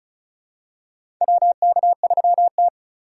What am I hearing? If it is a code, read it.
Morse code, heard as a keyed tone, about 36 wpm, 700 Hz. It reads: WK3T
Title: USA Call Signs - 4 Characters - 2x Character Spacing - 35wpm